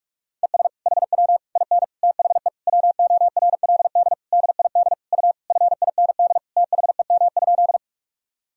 Morse code, heard as a keyed tone, about 47 wpm, 700 Hz. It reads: ES HW IN THE WORLD DID U FIND THEM?
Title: Sentences from Top 200 Words and Abbreviations 45wpm